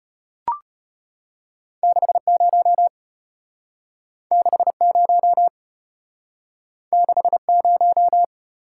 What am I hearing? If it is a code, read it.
Morse code, heard as a keyed tone, about 34 wpm, 700 Hz. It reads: 6 0 6 0 6 0